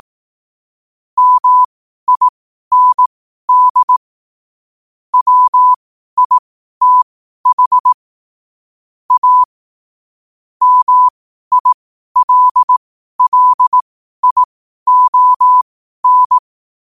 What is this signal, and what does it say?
Morse code, heard as a keyed tone, about 18 wpm, spaced as for 12 wpm, 1000 Hz. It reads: MIND WITH A MILLION
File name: qotdfiles / 12wpm